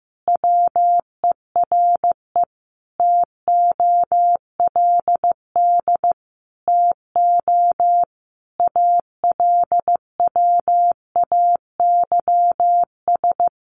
Morse code, heard as a keyed tone, about 15 wpm, 700 Hz. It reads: WERE TOLD TO ALWAYS
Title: Sentences from Top 1300 Words 15wpm